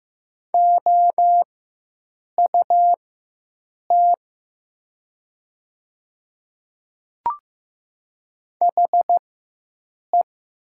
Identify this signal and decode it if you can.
Morse code, heard as a keyed tone, about 15 wpm, spaced as for 7 wpm, 700 Hz. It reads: OUT HE
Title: 4x Character Spacing - Top 100 Words - 15wpm